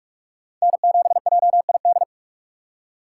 Morse code, heard as a keyed tone, about 46 wpm, 700 Hz. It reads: N7JID
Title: USA Call Signs - 6x Character Spacing - 45wpm